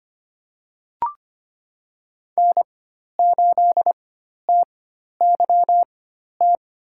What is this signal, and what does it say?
Morse code, heard as a keyed tone, about 25 wpm, spaced as for 12 wpm, 700 Hz. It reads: N8TYT
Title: USA Call Signs - 5 Characters - 4x Character Spacing - 25wpm